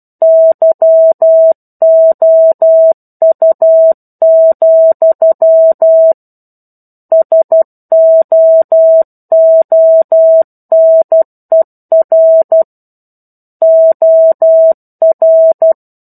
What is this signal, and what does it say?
Morse code, heard as a keyed tone, about 12 wpm, 650 Hz. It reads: YOU, SOONER OR
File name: 01-Chapter-01 12 WPM